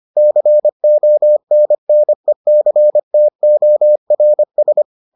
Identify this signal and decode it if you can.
Morse code, heard as a keyed tone, about 25 wpm, 600 Hz. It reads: CONNECTORS